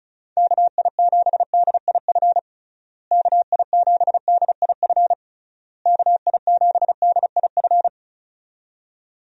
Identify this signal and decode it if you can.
Morse code, heard as a keyed tone, about 36 wpm, 700 Hz. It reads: KI7DIF KI7DIF KI7DIF
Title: USA Call Signs - 6 Characters - Repeated 3x - 35wpm